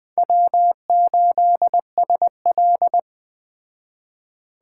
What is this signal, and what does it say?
Morse code, heard as a keyed tone, about 20 wpm, 700 Hz. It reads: W8SL